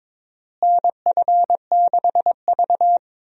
Morse code, heard as a keyed tone, about 22 wpm, 700 Hz. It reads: NF6V
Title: USA Call Signs - 4 Characters - Repeated 3x - 22wpm